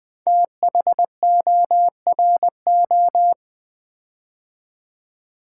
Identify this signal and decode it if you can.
Morse code, heard as a keyed tone, about 20 wpm, 700 Hz. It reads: THORO
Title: Build A Word - 5 Letters At A Time - 10-Letter Words - 20wpm